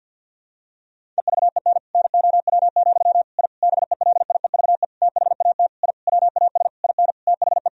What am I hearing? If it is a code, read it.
Morse code, heard as a keyed tone, about 46 wpm, 700 Hz. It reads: EVEN NOW, I BELIEVE THAT I WAS IN THE
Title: Sentences from Top 1000 Words - Review Part 1 - 50wpm